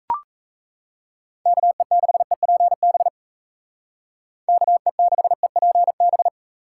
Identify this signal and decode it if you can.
Morse code, heard as a keyed tone, about 37 wpm, 700 Hz. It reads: KE6EPB KE6EPB